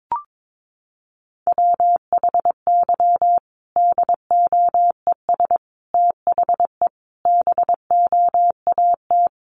Morse code, heard as a keyed tone, about 22 wpm, 700 Hz. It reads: WHY DOES THE BOAT
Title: Sentences from Top 500 Words 22wpm